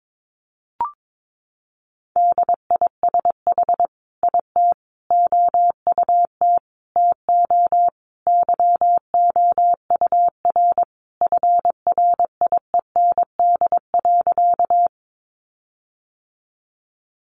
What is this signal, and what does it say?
Morse code, heard as a keyed tone, about 22 wpm, 700 Hz. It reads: DISH IT OUT TO YOUR FRIEND.
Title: Sentences from Top 1300 Words 22wpm